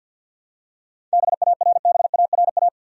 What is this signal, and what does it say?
Morse code, heard as a keyed tone, about 47 wpm, 700 Hz. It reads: BARBARA
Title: USA Common Names - 50wpm